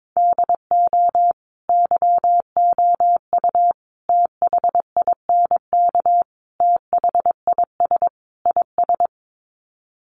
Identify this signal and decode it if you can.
Morse code, heard as a keyed tone, about 22 wpm, 700 Hz. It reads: DO YOU THINK THIS IS
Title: Sets of 5 Words Encore2 22wpm